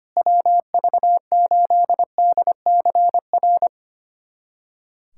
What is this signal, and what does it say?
Morse code, heard as a keyed tone, about 25 wpm, 700 Hz. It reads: WV8DCR
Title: USA Callsign DE Callsign 25wpm